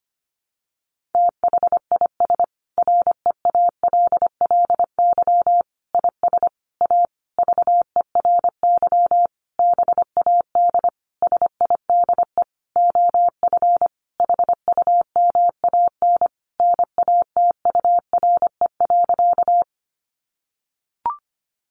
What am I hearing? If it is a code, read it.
Morse code, heard as a keyed tone, about 25 wpm, 700 Hz. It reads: THIS REALLY IS A VERY BAD SIDE OF HUMAN NATURE.
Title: Sentences from Top 700 Words 25wpm